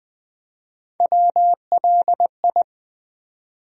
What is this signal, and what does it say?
Morse code, heard as a keyed tone, about 20 wpm, 700 Hz. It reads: WLI
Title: USA Call Sign Postfixes - 3 Letters 20wpm